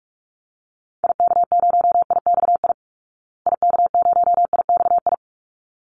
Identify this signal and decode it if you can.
Morse code, heard as a keyed tone, about 44 wpm, 700 Hz. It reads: IK0IXI IK0IXI
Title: Call Signs Repeated 3x - 45wpm